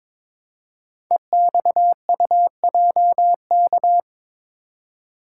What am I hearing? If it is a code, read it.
Morse code, heard as a keyed tone, about 23 wpm, 700 Hz. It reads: EXUJK
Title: Code Groups - 5 Letters - 22wpm